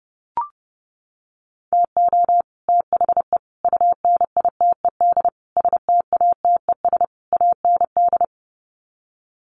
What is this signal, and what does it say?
Morse code, heard as a keyed tone, about 30 wpm, 700 Hz. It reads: TO THE UNITED STATES AND
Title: Sets of 5 Words Encore 30wpm